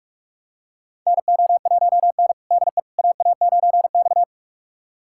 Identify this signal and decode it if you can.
Morse code, heard as a keyed tone, about 44 wpm, 700 Hz. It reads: NO1N DE AA9X
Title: USA Callsign DE Callsign - 4 Characters - 45wpm